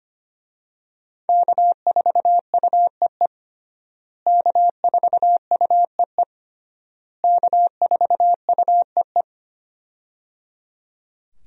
Morse code, heard as a keyed tone, about 25 wpm, 700 Hz. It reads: K4UEE K4UEE K4UEE
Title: USA Call Signs - 5 Characters - Repeated 3x - 25wpm